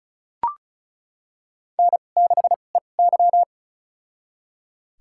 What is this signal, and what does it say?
Morse code, heard as a keyed tone, about 36 wpm, spaced as for 25 wpm, 700 Hz. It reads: N6EY
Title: USA Call Signs - 4 Characters - 2x Character Spacing - 35wpm